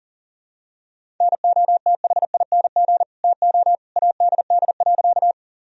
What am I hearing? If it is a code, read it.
Morse code, heard as a keyed tone, about 40 wpm, 700 Hz. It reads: NOTHING TO ADD.
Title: Sentences from Top 1000 Words - Review Part 1 - 40wpm